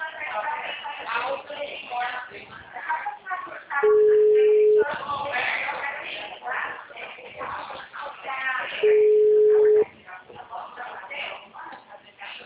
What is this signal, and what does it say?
Ringback tone (Europe, 425 Hz), with speech.